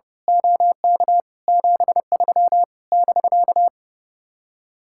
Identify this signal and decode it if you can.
Morse code, heard as a keyed tone, about 28 wpm, 700 Hz. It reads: OK 73 <BK>
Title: Speed-Racing Common QSO Phrases Encore - 30wpm